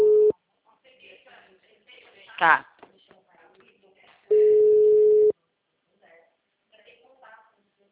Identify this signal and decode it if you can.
Ringback tone (Europe, 425 Hz), with speech.